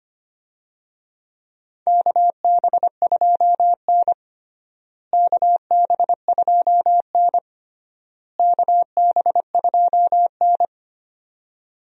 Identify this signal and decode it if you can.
Morse code, heard as a keyed tone, about 25 wpm, 700 Hz. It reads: KB2N KB2N KB2N
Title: USA Call Signs - 4 Characters - Repeated 3x - 25wpm